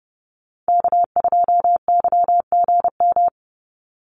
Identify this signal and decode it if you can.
Morse code, heard as a keyed tone, about 30 wpm, 700 Hz. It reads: K2YGM
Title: Call Signs - 30wpm